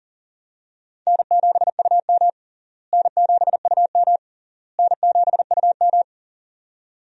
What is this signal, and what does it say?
Morse code, heard as a keyed tone, about 40 wpm, 700 Hz. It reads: N7UM N7UM N7UM
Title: USA Call Signs - 4 Characters - Repeated 3x - 40wpm